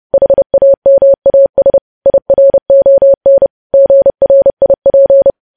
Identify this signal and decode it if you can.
Morse code, heard as a keyed tone, about 30 wpm, 550 Hz. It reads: HAMAS IRON GRIP
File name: podcast / News headlines-1x-30x30